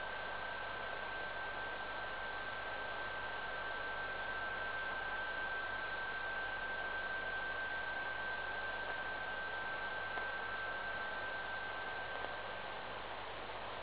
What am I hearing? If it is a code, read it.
PSK31, about 31.25 baud.